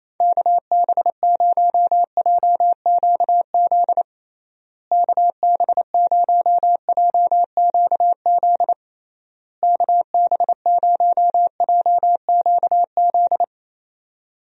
Morse code, heard as a keyed tone, about 28 wpm, 700 Hz. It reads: KB0JQZ KB0JQZ KB0JQZ
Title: USA Call Signs - 6 Characters - Repeated 3x - 28wpm